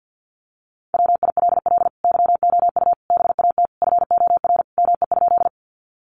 Morse code, heard as a keyed tone, about 50 wpm, 700 Hz. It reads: WILL YOU BAT FOR ME?
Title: Sentences from Top 1500 Words 50wpm